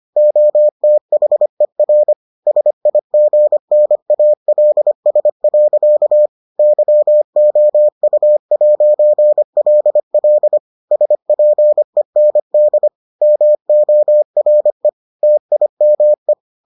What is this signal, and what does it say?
Morse code, heard as a keyed tone, about 25 wpm, 600 Hz. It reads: OTHER SIGNALS. YOU'LL SPEND MORE TIME